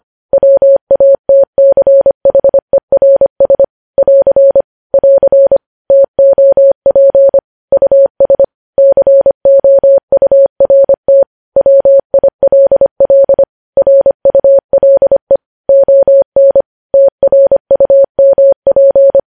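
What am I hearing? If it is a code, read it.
Morse code, heard as a keyed tone, about 25 wpm, 550 Hz. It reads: WATCHERS <AR> <AR> TOP US COURT WILL RULE ON TRUMP
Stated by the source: Morse news headlines sent at 25 wpm.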